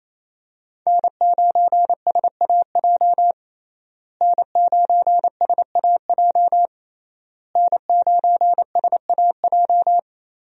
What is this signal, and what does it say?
Morse code, heard as a keyed tone, about 28 wpm, 700 Hz. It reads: N9SAJ N9SAJ N9SAJ
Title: USA Call Signs - 5 Characters - Repeated 3x - 28wpm